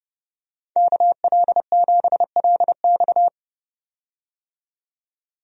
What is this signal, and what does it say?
Morse code, heard as a keyed tone, about 32 wpm, 700 Hz. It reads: KL7LX
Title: DXCC-100 Call Signs - 5 Characters - 30wpm